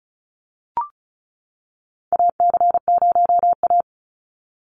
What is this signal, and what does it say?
Morse code, heard as a keyed tone, about 36 wpm, 700 Hz. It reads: AC0A
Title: NA SOTA Chaser Callsigns 35wpm